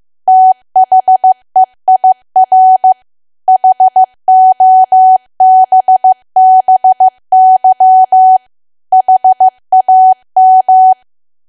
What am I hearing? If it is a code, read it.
Morse code, heard as a keyed tone, about 15 wpm, 750 Hz. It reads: THEIR HOBBY HAM
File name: cwa / W1AW-Level2 / 15WPM